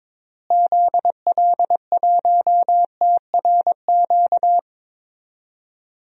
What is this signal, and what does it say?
Morse code, heard as a keyed tone, about 21 wpm, 700 Hz. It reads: ZL1TRQ
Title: DXCC-100 Call Signs - 6 Characters - Repeated 2x - 22wpm